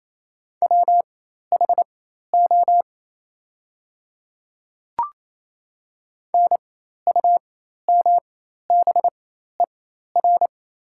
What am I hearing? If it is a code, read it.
Morse code, heard as a keyed tone, about 27 wpm, spaced as for 13 wpm, 700 Hz. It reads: WHO NUMBER
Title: 4x Character Spacing - Top 100 Words - 28wpm